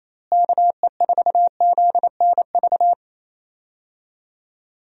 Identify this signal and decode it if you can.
Morse code, heard as a keyed tone, about 28 wpm, 700 Hz. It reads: KE4ZNV
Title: USA Call Signs - 6 Characters - Repeated 3x - 28wpm